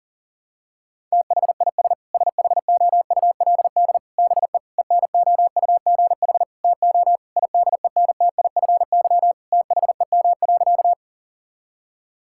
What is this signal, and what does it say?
Morse code, heard as a keyed tone, about 40 wpm, 700 Hz. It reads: THIS SHOULD BE ENOUGH TO IDENTIFY THEM.
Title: Sentences from Top 1000 Words - Review Part 6 - 40wpm